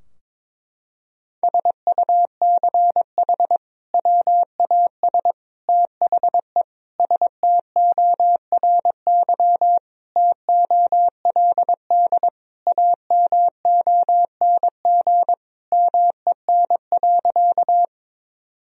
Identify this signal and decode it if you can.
Morse code, heard as a keyed tone, about 22 wpm, 700 Hz. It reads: SUCH WAS THE STORY TOLD AMONG MEN.
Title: Sentences from Top 1000 Words - Review Part 3 - 22wpm